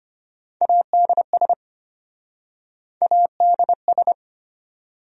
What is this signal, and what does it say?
Morse code, heard as a keyed tone, about 34 wpm, 700 Hz. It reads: ADS ADS